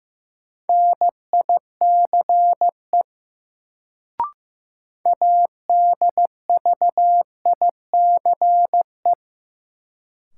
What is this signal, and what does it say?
Morse code, heard as a keyed tone, about 15 wpm, 700 Hz. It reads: NICE ADVICE
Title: Sentences from Top 1100 Words - New Word Drill - 15wpm